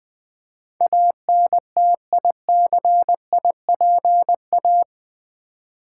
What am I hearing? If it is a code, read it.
Morse code, heard as a keyed tone, about 20 wpm, 700 Hz. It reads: ANTICIPA
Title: Build A Word - 4 Letters At A Time - 10-Letter Words - 20wpm